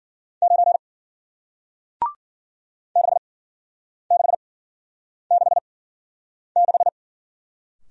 Morse code, heard as a keyed tone, about 53 wpm, spaced as for 9 wpm, 700 Hz. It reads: 9 6666